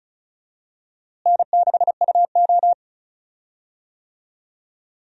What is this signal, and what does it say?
Morse code, heard as a keyed tone, about 34 wpm, 700 Hz. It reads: N6UO
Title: USA Call Signs - 4 Characters - Repeated 3x - 35wpm